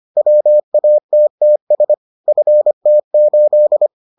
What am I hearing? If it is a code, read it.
Morse code, heard as a keyed tone, about 25 wpm, 600 Hz. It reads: WATTS FT8